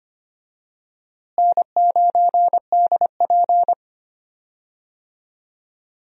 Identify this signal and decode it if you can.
Morse code, heard as a keyed tone, about 25 wpm, 700 Hz. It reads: N9DP
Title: USA Call Signs - 4 Characters - Repeated 3x - 25wpm